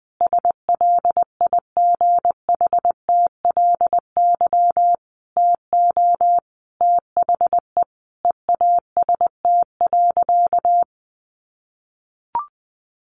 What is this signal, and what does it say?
Morse code, heard as a keyed tone, about 20 wpm, 700 Hz. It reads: SLIGHTLY TO THE EAST.
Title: Sentences from Top 1300 Words 20wpm